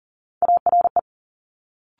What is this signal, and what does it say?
Morse code, heard as a keyed tone, about 38 wpm, 700 Hz. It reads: ARE
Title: Multi-letter Phonemes Quick Fire 40wpm